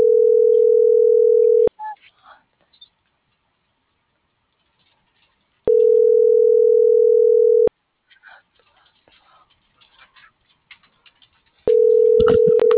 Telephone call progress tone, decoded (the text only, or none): ringback tone